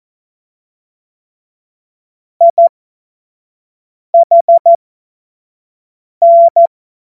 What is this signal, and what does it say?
Morse code, heard as a keyed tone, about 14 wpm, spaced as for 5 wpm, 680 Hz. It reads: IHN
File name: Witz-07-Oktober-2025-25-F70